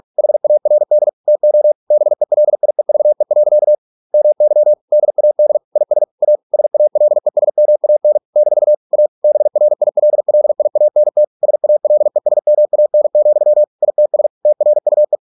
Morse code, heard as a keyed tone, about 46 wpm, 600 Hz. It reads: HARD TO BELIEVE. MY DAD IS A SALESMAN - A BRILLIANT SALESMAN, ITS TRUE